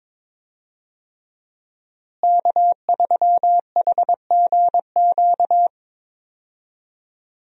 Morse code, heard as a keyed tone, about 22 wpm, 700 Hz. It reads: K3HGQ
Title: USA Call Signs - 5 Characters - Repeated 3x - 22wpm